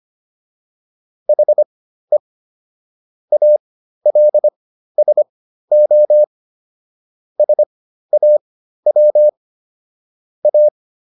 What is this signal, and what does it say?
Morse code, heard as a keyed tone, about 25 wpm, spaced as for 13 wpm, 600 Hz. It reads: HE ALSO SAW A
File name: SS109 13